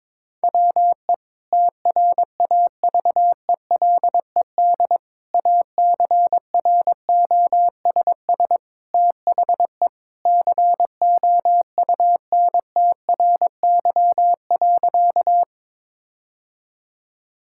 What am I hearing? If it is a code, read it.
Morse code, heard as a keyed tone, about 22 wpm, 700 Hz. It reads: WE TRAVELED ACROSS THE COUNTRY.
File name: practice / 22wpm / Sentences with 2-Word Phrases - 22wpm